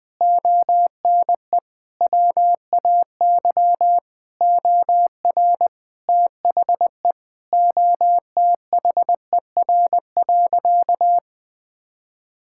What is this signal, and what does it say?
Morse code, heard as a keyed tone, about 20 wpm, 700 Hz. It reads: ONE WAY OR THE OTHER.
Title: Sentences with 5-Word Phrases - 20wpm